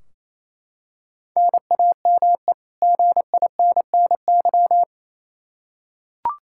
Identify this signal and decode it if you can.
Morse code, heard as a keyed tone, about 28 wpm, 700 Hz. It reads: NAME GINNY
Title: Practice QSO - Exchange 1 - Beginning - 28wpm